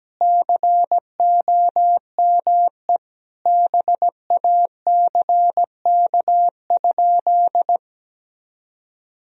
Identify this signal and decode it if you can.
Morse code, heard as a keyed tone, about 17 wpm, 700 Hz. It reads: COME BACK?
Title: Sentences from Top 300 Words and Abbreviations 17wpm